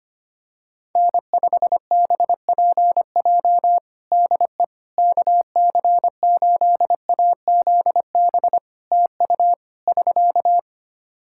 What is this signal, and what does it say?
Morse code, heard as a keyed tone, about 25 wpm, 700 Hz. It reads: N5BPJ DE KC8AZB TU <SK>
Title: K1USN SST QSO Practice 25wpm